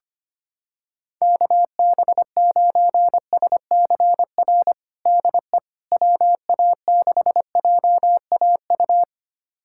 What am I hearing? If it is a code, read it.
Morse code, heard as a keyed tone, about 25 wpm, 700 Hz. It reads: KB9SCR DE WA6JAU
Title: USA Callsign DE Callsign - 6 Characters - 25wpm